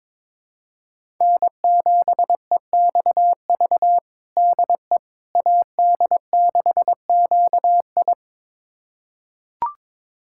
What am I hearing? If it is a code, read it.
Morse code, heard as a keyed tone, about 22 wpm, 700 Hz. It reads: N7EXV DE AD6QI
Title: USA Callsign DE Callsign - 5 Characters - 22wpm